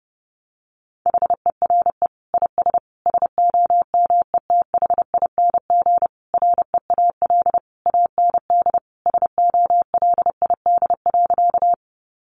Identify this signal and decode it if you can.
Morse code, heard as a keyed tone, about 30 wpm, 700 Hz. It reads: HERE IS SOMETHING REAL AND SOLID.
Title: Sentences from Top 1500 Words 30wpm